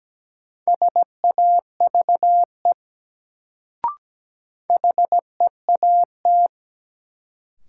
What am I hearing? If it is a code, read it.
Morse code, heard as a keyed tone, about 17 wpm, 700 Hz. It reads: SAVE HEAT
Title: Sentences from Top 1100 Words - New Word Drill Alt - 17wpm